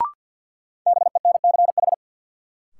Morse code, heard as a keyed tone, about 53 wpm, 700 Hz. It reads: BENCH
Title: Sentences from Top 1400 Words - New Word Drill Alt - 50wpm